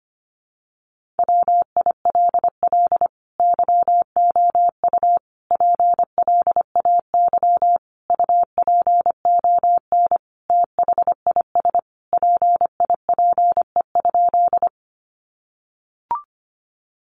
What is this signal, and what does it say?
Morse code, heard as a keyed tone, about 25 wpm, 700 Hz. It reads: WILL YOU PLAY UPON THIS PIPE?
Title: Sentences from Top 1500 Words 25wpm